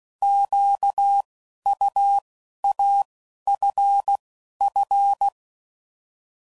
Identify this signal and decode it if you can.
Morse code, heard as a keyed tone, about 16 wpm, 800 Hz. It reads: Q U A F F
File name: Prefixes / By First Letter / Qs